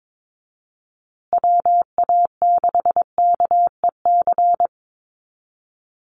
Morse code, heard as a keyed tone, about 22 wpm, 700 Hz. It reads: WA6KEC
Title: USA Call Signs Repeated 2x 22wpm